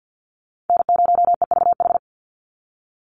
Morse code, heard as a keyed tone, about 48 wpm, 700 Hz. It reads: N0EVH
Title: NA SOTA Chaser Callsigns 50wpm